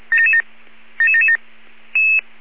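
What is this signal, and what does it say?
Five-tone selective call (ZVEI), heard as quick runs of five tones, tones of about 70 ms. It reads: 77777
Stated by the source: Five-tone ZVEI selective call received on SDR.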